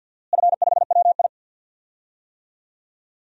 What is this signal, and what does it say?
Morse code, heard as a keyed tone, about 52 wpm, 700 Hz. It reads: F5WI
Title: DXCC-100 Call Signs - 4 Characters - Repeated 3x - 50wpm